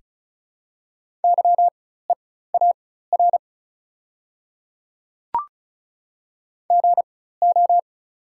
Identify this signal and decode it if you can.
Morse code, heard as a keyed tone, about 36 wpm, spaced as for 16 wpm, 690 Hz. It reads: YEAR GO
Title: 4x Character Spacing - Top 200 Words - 35wpm